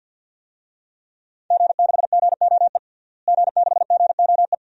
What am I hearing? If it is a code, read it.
Morse code, heard as a keyed tone, about 51 wpm, 700 Hz. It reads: G6GOE G6GOE
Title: DXCC-100 Call Signs - 5 Characters - Repeated 2x - 50wpm